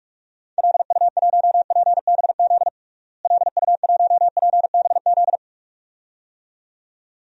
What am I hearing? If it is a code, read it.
Morse code, heard as a keyed tone, about 46 wpm, 700 Hz. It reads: LU1PBZ LU1PBZ
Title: DXCC-100 Call Signs - 6 Characters - Repeated 2x - 45wpm